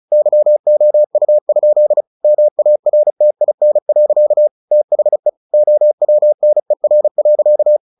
Morse code, heard as a keyed tone, about 34 wpm, 600 Hz. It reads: YOU? MARTIN. THE OWNER.
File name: the old man and the sea / the old man and the sea 1 35wpm